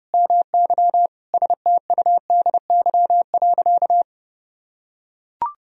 Morse code, heard as a keyed tone, about 30 wpm, 700 Hz. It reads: MY STUDY.
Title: Sentences from Top 1000 Words - Review Part 1 - 30wpm